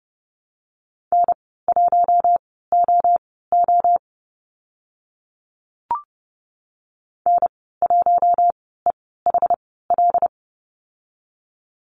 Morse code, heard as a keyed tone, about 28 wpm, spaced as for 17 wpm, 700 Hz. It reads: N1OO N1EHL
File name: Lesson 14g - Number 2 - Call Signs 30wpm